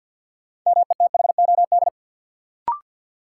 Morse code, heard as a keyed tone, about 53 wpm, 700 Hz. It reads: METHOD
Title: Sentences from Top 600 Words - New Word Drill - 50wpm